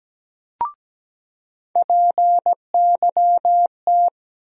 Morse code, heard as a keyed tone, about 17 wpm, 700 Hz. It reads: PYT
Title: USA Call Sign Postfixes - 3 Letters 17wpm